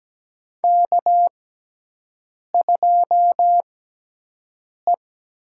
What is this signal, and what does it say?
Morse code, heard as a keyed tone, about 17 wpm, spaced as for 6 wpm, 700 Hz. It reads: K2E
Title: USA Call Signs - 5 Characters - 6x Character Spacing - 17wpm